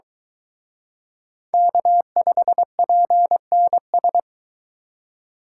Morse code, heard as a keyed tone, about 23 wpm, 700 Hz. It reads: K5PNS